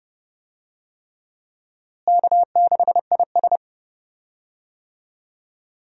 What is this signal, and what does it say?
Morse code, heard as a keyed tone, about 30 wpm, 700 Hz. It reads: K6IS
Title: USA Call Signs - 4 Characters - Repeated 3x - 30wpm